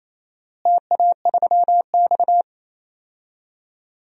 Morse code, heard as a keyed tone, about 28 wpm, 700 Hz. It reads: TA3X
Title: DXCC-100 Call Signs - 4 Characters - Repeated 3x - 28wpm